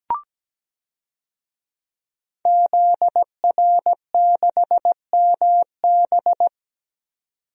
Morse code, heard as a keyed tone, about 17 wpm, 700 Hz. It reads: ZR6MB
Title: DXCC-100 Call Signs - 5 Characters - 17wpm